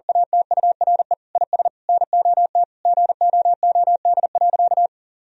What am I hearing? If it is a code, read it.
Morse code, heard as a keyed tone, about 40 wpm, 700 Hz. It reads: ATURE IS NOT GOOD.
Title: Sentences from Top 1000 Words - Review Part 4 - 40wpm